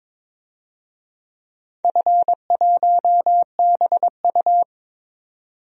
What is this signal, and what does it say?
Morse code, heard as a keyed tone, about 22 wpm, 700 Hz. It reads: F1BU
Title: DXCC-100 Call Signs - 4 Characters - 22wpm